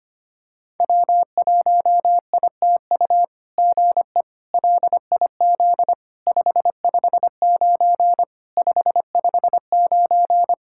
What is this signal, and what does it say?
Morse code, heard as a keyed tone, about 25 wpm, 700 Hz. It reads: W1ITU GE LIZ 559 559
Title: SOTA QSO Practice 25wpm